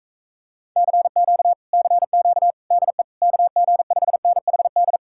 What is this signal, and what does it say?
Morse code, heard as a keyed tone, about 45 wpm, 700 Hz. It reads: CQ CQ DE KG5NHD